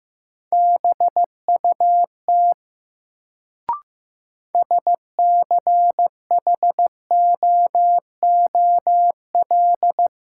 Morse code, heard as a keyed tone, about 15 wpm, 700 Hz. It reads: BUT SCHOOL
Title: Sets of 2 Words - New Word Drill - 15wpm